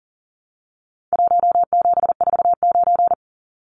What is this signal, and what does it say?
Morse code, heard as a keyed tone, about 40 wpm, 700 Hz. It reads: 1749
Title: Numbers - Contest Style - 4-Digit 40wpm